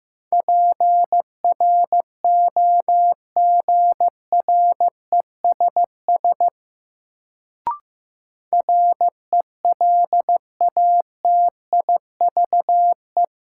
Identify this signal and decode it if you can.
Morse code, heard as a keyed tone, about 15 wpm, 700 Hz. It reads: PROGRESS RELATIVE
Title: Sentences from Top 1400 Words - New Word Drill Alt - 15wpm